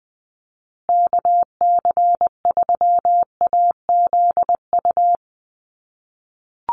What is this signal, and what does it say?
Morse code, heard as a keyed tone, about 20 wpm, 700 Hz. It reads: KC3AZU
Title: USA Call Signs Repeated 2x 20wpm